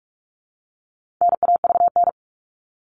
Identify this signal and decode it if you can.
Morse code, heard as a keyed tone, about 44 wpm, 700 Hz. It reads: NAVN
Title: All Cut Numbers - 4-Digit 45wpm